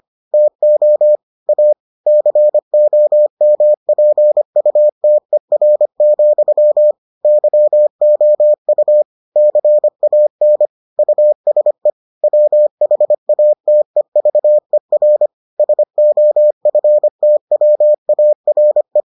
Morse code, heard as a keyed tone, about 25 wpm, 600 Hz. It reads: TO A COMPUTER, YOU CAN USE WHATEVER SOFTWARE